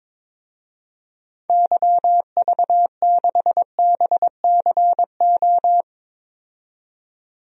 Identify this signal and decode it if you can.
Morse code, heard as a keyed tone, about 22 wpm, 700 Hz. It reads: YV6BCO
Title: DXCC-100 Call Signs - 6 Characters - 22wpm